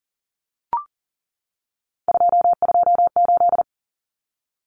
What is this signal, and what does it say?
Morse code, heard as a keyed tone, about 40 wpm, 700 Hz. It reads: 228
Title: Common Cut Numbers - 3-Digit 40wpm